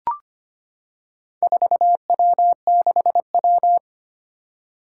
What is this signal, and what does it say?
Morse code, heard as a keyed tone, about 25 wpm, 700 Hz. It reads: 4W6W
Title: Four Challenging Letter-Number Combinations 25wpm